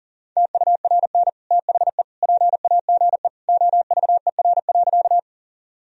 Morse code, heard as a keyed tone, about 37 wpm, 700 Hz. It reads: TURN THE PAGE OVER.
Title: Sentences from Top 1000 Words - Review Part 1 - 40wpm